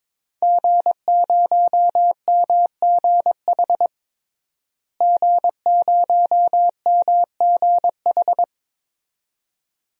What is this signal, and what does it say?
Morse code, heard as a keyed tone, about 22 wpm, 700 Hz. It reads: G0MGH G0MGH
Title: DXCC-100 Call Signs - 5 Characters - Repeated 2x - 22wpm